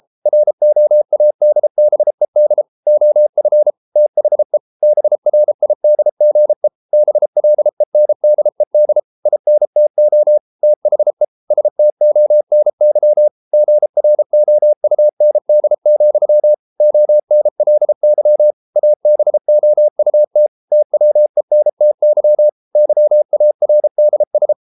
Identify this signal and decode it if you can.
Morse code, heard as a keyed tone, about 33 wpm, 600 Hz. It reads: ROADBED OF THE BRIDGE BLENDED INTO THE STONY GROUND, ONLY ABOUT TWENTY YARDS